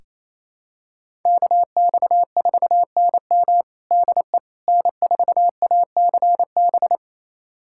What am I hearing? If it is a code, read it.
Morse code, heard as a keyed tone, about 28 wpm, 700 Hz. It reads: KX4NM DE N4ACB
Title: USA Callsign DE Callsign 28wpm